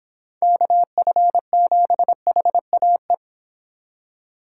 Morse code, heard as a keyed tone, about 29 wpm, 700 Hz. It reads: KF7HAE